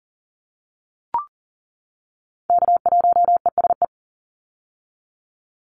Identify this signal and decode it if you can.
Morse code, heard as a keyed tone, about 40 wpm, 700 Hz. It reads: K1ESE
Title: Call Signs - 40wpm